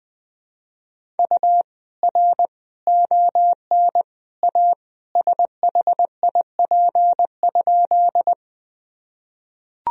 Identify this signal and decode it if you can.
Morse code, heard as a keyed tone, about 20 wpm, 700 Hz. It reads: U R ON A SHIP?
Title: Sentences from Top 400 Words and Abbreviations 20wpm